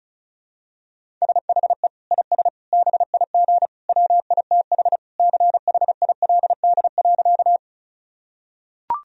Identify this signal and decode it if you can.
Morse code, heard as a keyed tone, about 34 wpm, 700 Hz. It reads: SHE IS BIG WITH CHILD.
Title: Sentences from Top 1000 Words - Review Part 1 - 35wpm